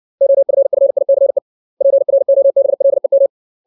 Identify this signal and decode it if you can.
Morse code, heard as a keyed tone, about 60 wpm, 550 Hz. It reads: OFFICE PROBLEM
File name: 632 words / 632 Words 60WPM